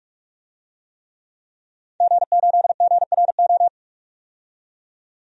Morse code, heard as a keyed tone, about 46 wpm, 700 Hz. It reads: G8GRO
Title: DXCC-100 Call Signs - 5 Characters - 45wpm